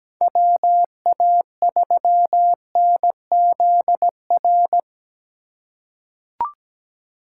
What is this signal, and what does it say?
Morse code, heard as a keyed tone, about 17 wpm, 700 Hz. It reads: WA3NZR
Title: K1USN SST QSO Practice - 17wpm